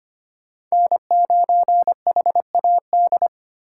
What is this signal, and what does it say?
Morse code, heard as a keyed tone, about 33 wpm, 700 Hz. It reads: N9HAD